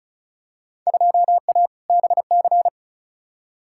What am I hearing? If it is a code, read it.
Morse code, heard as a keyed tone, about 36 wpm, 700 Hz. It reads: 2A BC
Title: Field Day Exchanges - 35wpm